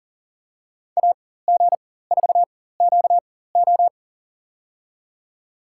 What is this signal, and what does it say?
Morse code, heard as a keyed tone, about 40 wpm, spaced as for 19 wpm, 700 Hz. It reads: AG4QO
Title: USA Call Signs - 5 Characters - 4x Character Spacing - 40wpm